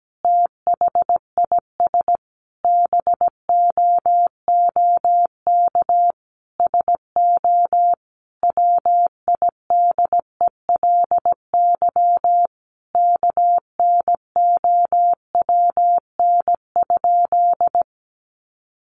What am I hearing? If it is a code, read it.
Morse code, heard as a keyed tone, about 17 wpm, 700 Hz. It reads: THIS BOOK SO WIDELY KNOWN?
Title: Sentences from Top 1200 Words 17wpm